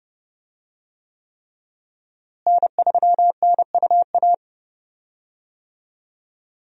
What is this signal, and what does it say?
Morse code, heard as a keyed tone, about 30 wpm, 700 Hz. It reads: N3NUA
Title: USA Call Signs - 5 Characters - Repeated 3x - 30wpm